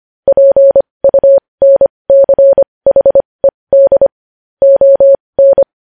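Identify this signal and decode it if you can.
Morse code, heard as a keyed tone, about 25 wpm, spaced as for 20 wpm, 550 Hz. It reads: PUNCHED ON
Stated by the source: Morse code news headlines practice at 20 wpm.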